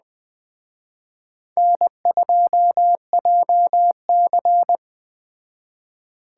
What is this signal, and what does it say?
Morse code, heard as a keyed tone, about 20 wpm, 700 Hz. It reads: N2JC
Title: Speed Racing - USA 1x2 Call Signs 17wpm